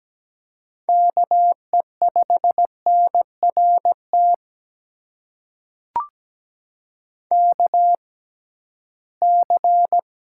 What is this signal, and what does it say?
Morse code, heard as a keyed tone, about 17 wpm, spaced as for 13 wpm, 700 Hz. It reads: KE5NRT KC
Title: USA Call Signs - 6 Characters - 6x Character Spacing - 17wpm